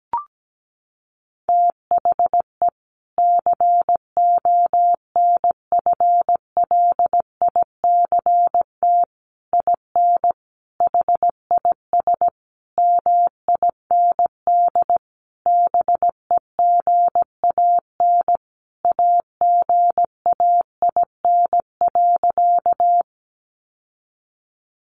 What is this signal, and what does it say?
Morse code, heard as a keyed tone, about 17 wpm, 700 Hz. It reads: THE CONFLICT IN HIS MIND BEGAN AGAIN.
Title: Sentences from Top 1300 Words 17wpm